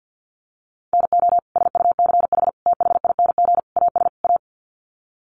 Morse code, heard as a keyed tone, about 50 wpm, 700 Hz. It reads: NO SUCH THING AS A
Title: Sets of 5 Words Encore 50wpm